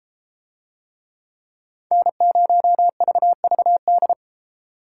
Morse code, heard as a keyed tone, about 33 wpm, 700 Hz. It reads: N0VVD